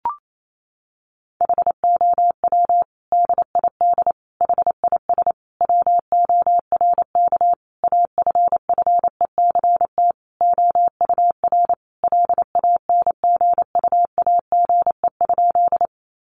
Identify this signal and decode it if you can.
Morse code, heard as a keyed tone, about 28 wpm, 700 Hz. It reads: HOW DID HIS WORK AFFECT OUR LANGUAGE?
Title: Sentences from Top 900 Words 28wpm